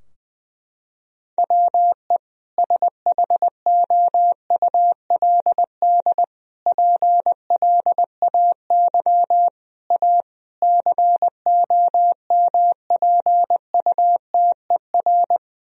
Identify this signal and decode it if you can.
Morse code, heard as a keyed tone, about 20 wpm, 700 Hz. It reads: WE SHOULD PLAY A COMPUTER
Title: Sentences from Top 1000 Words - Review Part 3 - 20wpm